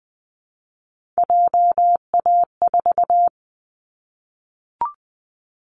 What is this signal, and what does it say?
Morse code, heard as a keyed tone, about 20 wpm, 700 Hz. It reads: JA4
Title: Callsign Prefixes for Top 25 DXCC - Repeated 3x - 20wpm